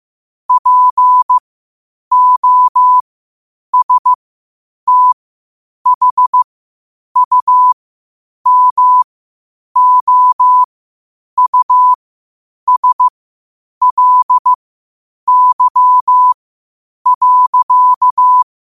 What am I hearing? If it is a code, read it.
Morse code, heard as a keyed tone, about 15 wpm, spaced as for 8 wpm, 1000 Hz. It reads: POSTHUMOUSLY.
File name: qotdfiles / 8wpm